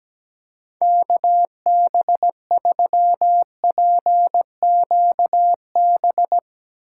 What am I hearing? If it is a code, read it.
Morse code, heard as a keyed tone, about 20 wpm, 700 Hz. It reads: KB3PQB
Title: Speed Racing - USA 2x3 Call Signs 17wpm